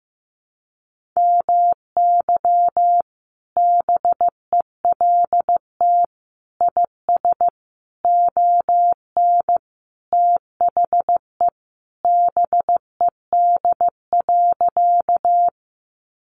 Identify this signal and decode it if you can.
Morse code, heard as a keyed tone, about 15 wpm, 700 Hz. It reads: MY BELT IS ON THE BED.
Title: Sentences from Top 1400 Words 15wpm